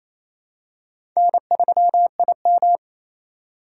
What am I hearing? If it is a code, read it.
Morse code, heard as a keyed tone, about 28 wpm, 700 Hz. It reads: N3IM
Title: USA Call Signs - 3x Character Spacing - 28wpm